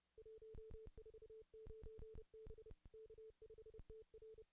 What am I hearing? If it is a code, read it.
Morse code, heard as a keyed tone, about 31 wpm, 440 Hz. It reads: 149 B K5TR
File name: contests / ss-cw-k5tr-op-n5rz -2200